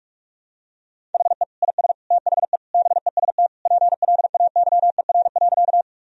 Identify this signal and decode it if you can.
Morse code, heard as a keyed tone, about 43 wpm, 700 Hz. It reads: HE IS THE BEST PLAYER.
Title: Sentences from Top 1000 Words - Review Part 4 - 45wpm